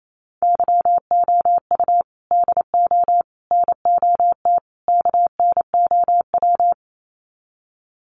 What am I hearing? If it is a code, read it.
Morse code, heard as a keyed tone, about 28 wpm, 700 Hz. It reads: YOU DO NOT KNOW
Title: Sets of 4 Words 28wpm